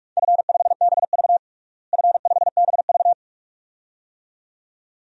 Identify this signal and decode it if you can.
Morse code, heard as a keyed tone, about 46 wpm, 700 Hz. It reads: F5BV F5BV
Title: DXCC-100 Call Signs - 4 Characters - Repeated 2x - 45wpm